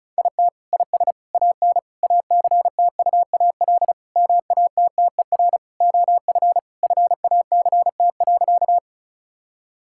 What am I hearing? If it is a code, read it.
Morse code, heard as a keyed tone, about 36 wpm, 700 Hz. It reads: IT IS AN ACTUAL MATTER OF FACT.
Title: Sentences from Top 1000 Words - Review Part 5 - 35wpm